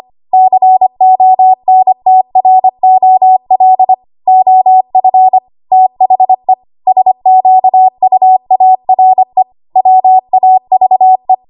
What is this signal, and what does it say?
Morse code, heard as a keyed tone, about 25 wpm, 750 Hz. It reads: CONTROL OF THE SQUARE WAVE
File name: Archive / 25 WPM / 25WPM